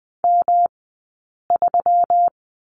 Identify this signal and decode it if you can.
Morse code, heard as a keyed tone, about 20 wpm, 700 Hz. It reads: M 3
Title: 120 Lesson 20d - Number 3 - Two Characters 20wpm